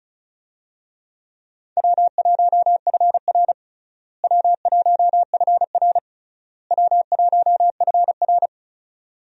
Morse code, heard as a keyed tone, about 34 wpm, 700 Hz. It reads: W1FR W1FR W1FR
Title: USA Call Signs - 4 Characters - Repeated 3x - 35wpm